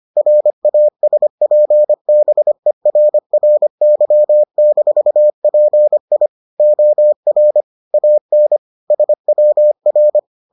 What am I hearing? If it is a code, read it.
Morse code, heard as a keyed tone, about 25 wpm, 600 Hz. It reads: RASPBERRY-PI OR AN SWR